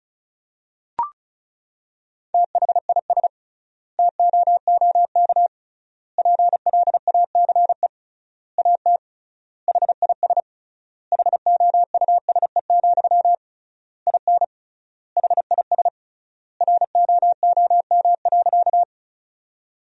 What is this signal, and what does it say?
Morse code, heard as a keyed tone, about 34 wpm, 700 Hz. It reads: THIS TOOK PLACE AT HIS HOUSE, IN HIS ROOM.
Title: 3x Word-Spacing - Sentences from Top 300 Words - 35wpm